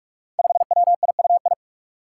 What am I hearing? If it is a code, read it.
Morse code, heard as a keyed tone, about 43 wpm, 700 Hz. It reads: 5WIUI
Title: Five Challenging Letter-Number Combinations 45wpm